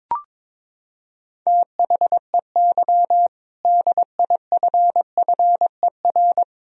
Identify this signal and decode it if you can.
Morse code, heard as a keyed tone, about 22 wpm, 700 Hz. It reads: THEY DIFFER
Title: Sentences from Top 1000 Words - Review Part 2 - 22wpm